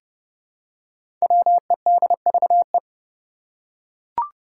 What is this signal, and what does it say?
Morse code, heard as a keyed tone, about 31 wpm, 700 Hz. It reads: WEDVE
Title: Five Challenging Letter-Number Combinations 30wpm